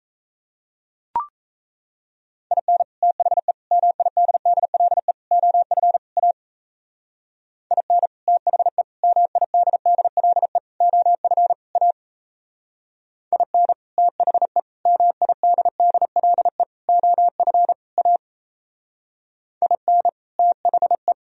Morse code, heard as a keyed tone, about 33 wpm, 700 Hz. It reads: IN THE MIDDLE OF A IN THE MIDDLE OF A IN THE MIDDLE OF A IN THE
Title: Speed-Racing - Sets of 5 Words 28wpm